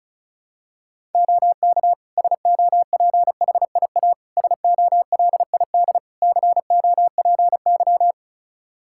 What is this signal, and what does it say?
Morse code, heard as a keyed tone, about 34 wpm, 700 Hz. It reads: OK SOPHIA SOLID COPY
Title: Practice QSO - Exchange 3 - More Info - 35wpm